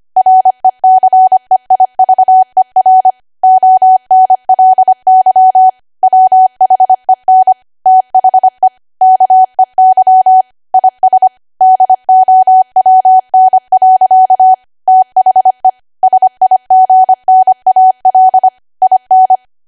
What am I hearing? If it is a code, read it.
Morse code, heard as a keyed tone, about 25 wpm, 750 Hz. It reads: RECEIVER ONLY WHEN THE KEY IS DOWN. THE SIGNAL IN
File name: Archive / 25 WPM / 25WPM